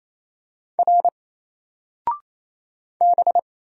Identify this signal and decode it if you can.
Morse code, heard as a keyed tone, about 27 wpm, 700 Hz. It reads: R B
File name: Single-Letter-Number-28wpm